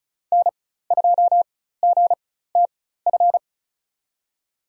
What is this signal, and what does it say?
Morse code, heard as a keyed tone, about 36 wpm, spaced as for 16 wpm, 700 Hz. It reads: N2GTF
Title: USA Call Signs - 5 Characters - 4x Character Spacing - 35wpm